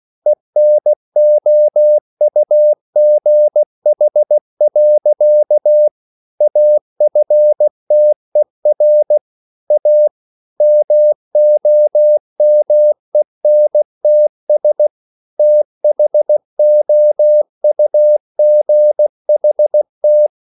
Morse code, heard as a keyed tone, about 16 wpm, 600 Hz. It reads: ENOUGH. AFTER A MOMENTS THOUGHT